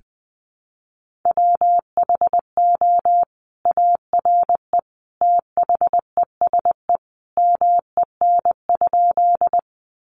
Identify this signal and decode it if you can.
Morse code, heard as a keyed tone, about 20 wpm, 700 Hz. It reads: WHO ARE THESE MEN?
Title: Sentences from Top 200 Words 20wpm